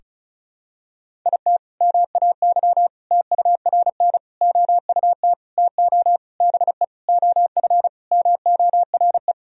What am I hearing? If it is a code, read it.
Morse code, heard as a keyed tone, about 36 wpm, 700 Hz. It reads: IT MAY TURN OUT TO BE OF MORE
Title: Sentences from Top 1000 Words - Review Part 2 - 35wpm